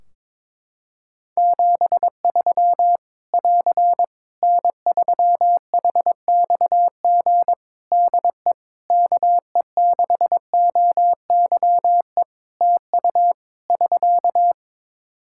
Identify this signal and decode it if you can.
Morse code, heard as a keyed tone, about 22 wpm, 700 Hz. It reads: 73 <AR> N3HXG DE KE6OYE TU <SK>
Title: Practice QSO - Exchange 4 - Ending - 22wpm